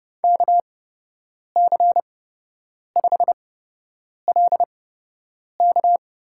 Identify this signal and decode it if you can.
Morse code, heard as a keyed tone, about 32 wpm, spaced as for 8 wpm, 700 Hz. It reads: KC5LK
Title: USA Call Signs - 8x Character Spacing - 30wpm